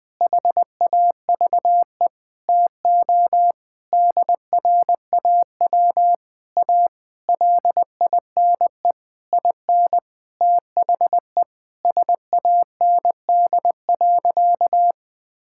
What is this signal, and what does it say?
Morse code, heard as a keyed tone, about 20 wpm, 700 Hz. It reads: HAVE TO DRAW A LINE IN THE SAND.
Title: Sentences with 7-Word Phrases - 20wpm